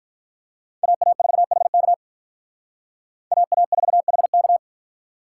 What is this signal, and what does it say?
Morse code, heard as a keyed tone, about 43 wpm, 700 Hz. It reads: AA4HK AA4HK
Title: Speed Racing - USA Call Signs 35wpm